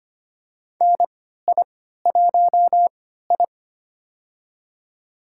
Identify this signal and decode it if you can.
Morse code, heard as a keyed tone, about 25 wpm, spaced as for 14 wpm, 700 Hz. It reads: NI1I
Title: USA Call Signs - 4 Characters - 3x Character Spacing - 25wpm